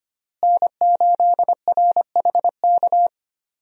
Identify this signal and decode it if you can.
Morse code, heard as a keyed tone, about 25 wpm, 700 Hz. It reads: N8RHK
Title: USA Call Signs - 5 Characters - Repeated 3x - 25wpm